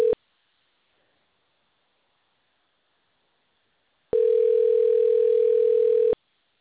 Ringback tone (North America), 440 + 480 Hz.